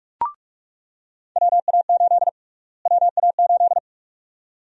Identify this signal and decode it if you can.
Morse code, heard as a keyed tone, about 42 wpm, 700 Hz. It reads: WA8 WA8
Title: USA Call Sign Prefixes - Repeated 2x - 45wpm